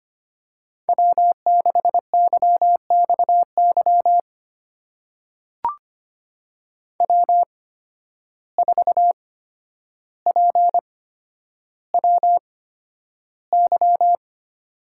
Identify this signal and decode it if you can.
Morse code, heard as a keyed tone, about 25 wpm, spaced as for 11 wpm, 700 Hz. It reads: W6YXY W4PWY
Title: USA Call Signs - 5 Characters - 8x Character Spacing - 25wpm